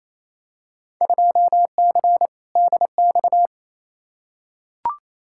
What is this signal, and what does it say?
Morse code, heard as a keyed tone, about 28 wpm, 700 Hz. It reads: 2C DX
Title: Field Day Exchanges Encore - 28wpm